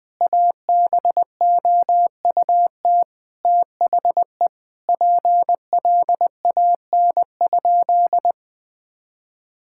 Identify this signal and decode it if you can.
Morse code, heard as a keyed tone, about 20 wpm, 700 Hz. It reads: ABOUT THE PLAN?
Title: Sentences with 5-Word Phrases - 20wpm